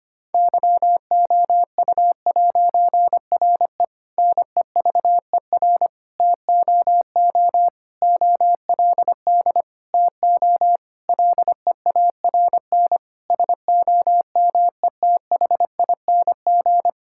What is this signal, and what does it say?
Morse code, heard as a keyed tone, about 25 wpm, 700 Hz. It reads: YOU'RE NEVER TOO OLD TO LEARN SOMETHING
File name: practice / 25wpm / Difficult Sentences with Hints - 25wpm